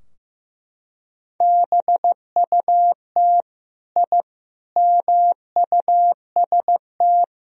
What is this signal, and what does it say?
Morse code, heard as a keyed tone, about 15 wpm, 700 Hz. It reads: BUT I MUST
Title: Sentences from Top 1000 Words - Review Part 3 - 15wpm